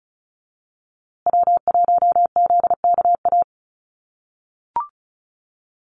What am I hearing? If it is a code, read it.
Morse code, heard as a keyed tone, about 34 wpm, 700 Hz. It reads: W1ZKA
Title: USA Call Signs Repeated 3x 35wpm